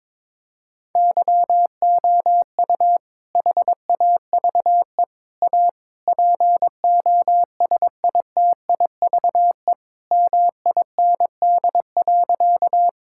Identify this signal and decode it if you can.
Morse code, heard as a keyed tone, about 22 wpm, 700 Hz. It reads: YOU HAVE A POSITIVE MIND.
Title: Sentences from Top 1000 Words - Review Part 5 - 22wpm